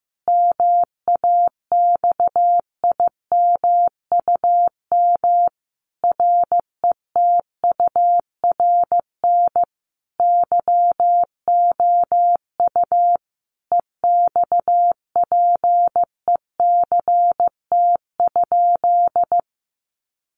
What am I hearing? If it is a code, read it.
Morse code, heard as a keyed tone, about 15 wpm, 700 Hz. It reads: MAXIMUM RETURN YOU EXPECT?
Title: Sentences from Top 1300 Words 15wpm